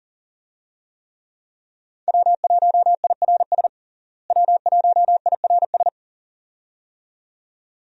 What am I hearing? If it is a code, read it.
Morse code, heard as a keyed tone, about 40 wpm, 700 Hz. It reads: W1IRS W1IRS
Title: USA Call Signs - 5 Characters - Repeated 2x - 40wpm